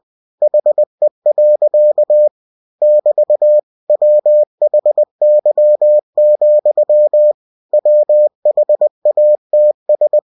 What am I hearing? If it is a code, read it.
Morse code, heard as a keyed tone, about 20 wpm, 600 Hz. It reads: HE. <BT> WHY, WHATS
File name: cw-books / winnie-the-pooh / winnie-20wpm-page-0045